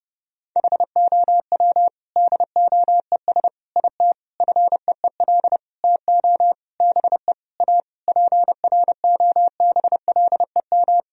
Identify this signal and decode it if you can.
Morse code, heard as a keyed tone, about 30 wpm, 700 Hz. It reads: HOW DOES IT FEEL TO BE A PROBLEM
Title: Sentences from Top 1000 Words - Review Part 2 - 30wpm